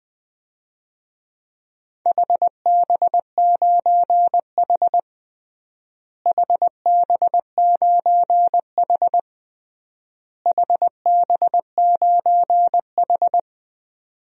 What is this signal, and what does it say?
Morse code, heard as a keyed tone, about 20 wpm, 700 Hz. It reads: HB9H HB9H HB9H
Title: DXCC-100 Call Signs - 4 Characters - Repeated 3x - 20wpm